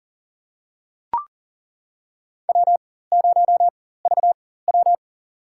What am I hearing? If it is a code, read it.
Morse code, heard as a keyed tone, about 40 wpm, spaced as for 19 wpm, 700 Hz. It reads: W0VW
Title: USA Call Signs - 4 Characters - 4x Character Spacing - 40wpm